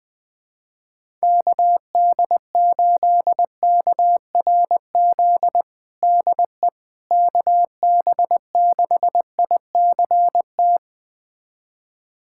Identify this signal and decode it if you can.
Morse code, heard as a keyed tone, about 19 wpm, 700 Hz. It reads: KD8KRZ DE KB6ICT
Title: USA Callsign DE Callsign 20wpm